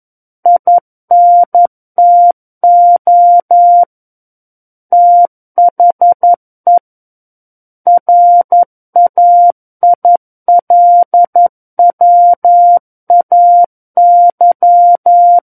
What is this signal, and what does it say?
Morse code, heard as a keyed tone, about 11 wpm, 700 Hz. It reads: INTO THE RAILWAY
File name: Book One / 02-Chapter 2 - 11 WPM